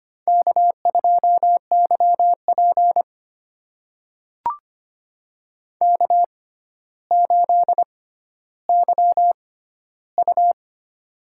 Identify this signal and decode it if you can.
Morse code, heard as a keyed tone, about 25 wpm, spaced as for 13 wpm, 700 Hz. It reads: K2YP K8YU
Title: USA Call Signs - 4 Characters - 6x Character Spacing - 25wpm